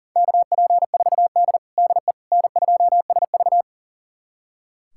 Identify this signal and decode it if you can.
Morse code, heard as a keyed tone, about 40 wpm, 700 Hz. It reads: KP4D DE N2SV
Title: USA Callsign DE Callsign - 4 Characters - 40wpm